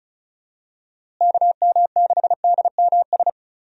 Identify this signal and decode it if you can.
Morse code, heard as a keyed tone, about 34 wpm, 700 Hz. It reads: KM6DMS
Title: USA Call Signs - 6 Characters - Repeated 3x - 35wpm